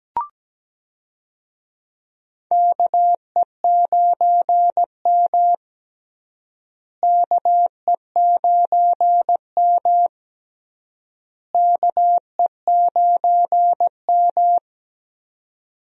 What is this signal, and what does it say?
Morse code, heard as a keyed tone, about 17 wpm, 700 Hz. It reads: KE9M KE9M KE9M
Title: USA Call Signs - 4 Characters - Repeated 3x - 17wpm